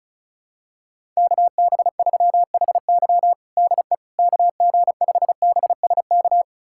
Morse code, heard as a keyed tone, about 36 wpm, 700 Hz. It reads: KB3HY DE KG5BSK
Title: USA Callsign DE Callsign 35wpm